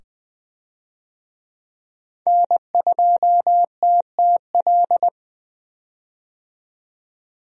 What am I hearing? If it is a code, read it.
Morse code, heard as a keyed tone, about 21 wpm, 700 Hz. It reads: N2TTL
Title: USA Call Signs - 5 Characters - 20wpm